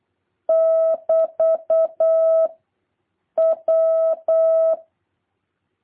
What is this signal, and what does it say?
Morse code, heard as a keyed tone, about 8 wpm, 650 Hz. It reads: <BT> W
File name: jh8cbh / 2023cw